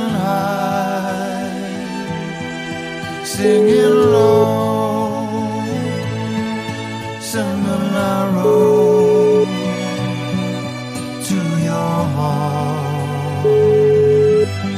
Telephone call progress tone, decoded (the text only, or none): ringback tone